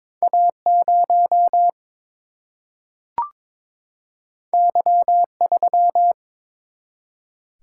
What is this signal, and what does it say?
Morse code, heard as a keyed tone, about 22 wpm, 700 Hz. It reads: A 0 Y 3
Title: Single-Letter Single-Number 22wpm